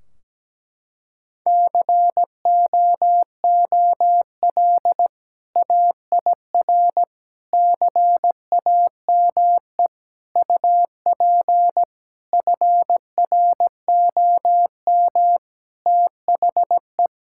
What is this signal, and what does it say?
Morse code, heard as a keyed tone, about 17 wpm, 700 Hz. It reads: COOL AIR CAME UP FROM THE
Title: Sentences from Top 1600 Words 17wpm